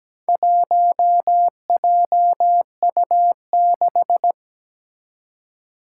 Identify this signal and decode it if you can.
Morse code, heard as a keyed tone, about 17 wpm, 700 Hz. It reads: 1JU6
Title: Four Challenging Letter-Number Combinations 17wpm